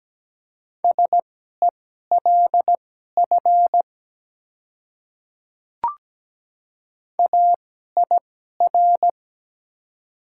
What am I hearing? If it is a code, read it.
Morse code, heard as a keyed tone, about 17 wpm, spaced as for 12 wpm, 700 Hz. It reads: SELF AIR
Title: 2x Character Spacing - Top 200 Words - 17wpm